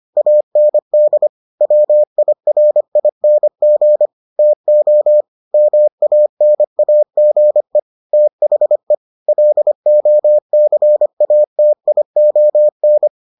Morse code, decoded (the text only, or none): AND WIRING TO MANAGE THE LOCATION